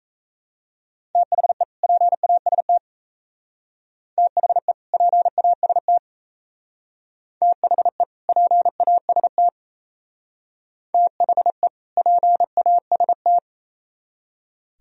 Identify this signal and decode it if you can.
Morse code, heard as a keyed tone, about 36 wpm, 700 Hz. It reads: THE PAST THE PAST THE PAST THE PAST